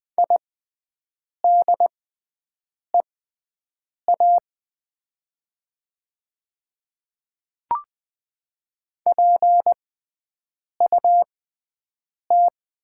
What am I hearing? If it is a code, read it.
Morse code, heard as a keyed tone, about 20 wpm, spaced as for 7 wpm, 700 Hz. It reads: IDEA PUT
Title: 6x Character Spacing - Top 300 Words - 20wpm